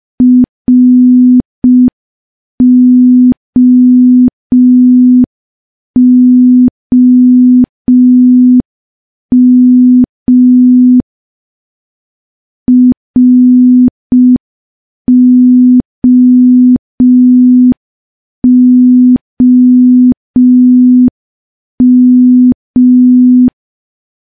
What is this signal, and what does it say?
Morse code, heard as a keyed tone, about 5 wpm, 250 Hz. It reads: ROOM ROOM
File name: morse / morse-training-05WPM